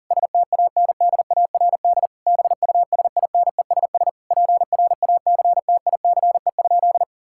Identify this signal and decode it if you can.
Morse code, heard as a keyed tone, about 40 wpm, 700 Hz. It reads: STANDARD BUSINESS PRACTICE?
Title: Sentences from Top 1000 Words - Review Part 6 - 40wpm